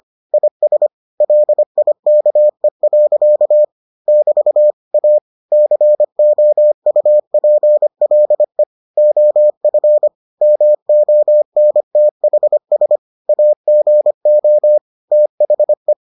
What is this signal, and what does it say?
Morse code, heard as a keyed tone, about 25 wpm, 600 Hz. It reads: IS LIKE. <BT> A COUPLE OF MONTHS AGO THE